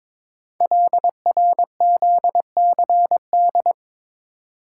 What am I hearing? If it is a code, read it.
Morse code, heard as a keyed tone, about 22 wpm, 700 Hz. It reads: LRZCD
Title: Code Groups - 5 Letters - 22wpm